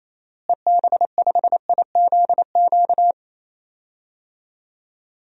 Morse code, heard as a keyed tone, about 28 wpm, 700 Hz. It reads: EB5IZQ
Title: DXCC-100 Call Signs - 6 Characters - Repeated 3x - 28wpm